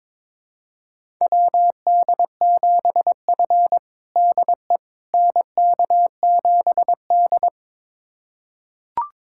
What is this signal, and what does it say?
Morse code, heard as a keyed tone, about 21 wpm, 700 Hz. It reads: WD7F DE NK7D
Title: USA Callsign DE Callsign - 4 Characters - 22wpm